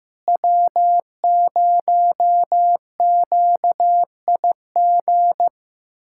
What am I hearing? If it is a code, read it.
Morse code, heard as a keyed tone, about 15 wpm, 700 Hz. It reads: W0QIG
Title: USA Callsign DE Callsign - 5 Characters - 15wpm